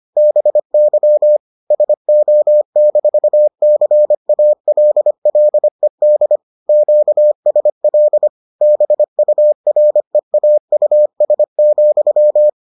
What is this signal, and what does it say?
Morse code, heard as a keyed tone, about 25 wpm, 600 Hz. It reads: BY SO-CALLED QSL BUREAUS,